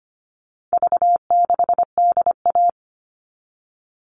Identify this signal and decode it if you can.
Morse code, heard as a keyed tone, about 25 wpm, 700 Hz. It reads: V6DA
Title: All Cut Numbers - 4-Digit 25wpm